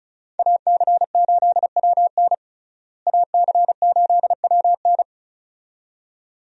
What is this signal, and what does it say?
Morse code, heard as a keyed tone, about 36 wpm, 700 Hz. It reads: AC8WN AC8WN
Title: USA Call Signs - 5 Characters - Repeated 2x - 35wpm